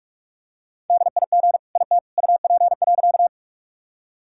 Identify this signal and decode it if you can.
Morse code, heard as a keyed tone, about 43 wpm, 700 Hz. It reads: DIG IT UP.
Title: Sentences from Top 1600 Words 45wpm